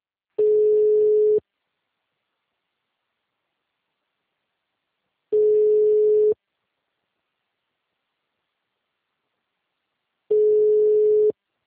Ringback tone (Europe, 425 Hz).
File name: out- -2019- - - .44